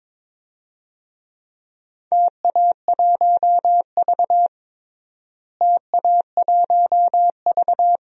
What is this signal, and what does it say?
Morse code, heard as a keyed tone, about 22 wpm, 700 Hz. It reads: TA1V TA1V
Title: DXCC-100 Call Signs - 4 Characters - Repeated 3x - 22wpm